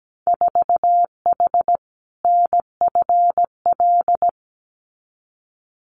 Morse code, heard as a keyed tone, about 17 wpm, 700 Hz. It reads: 4H NFL
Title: Winter Field Day Exchanges Encore 17wpm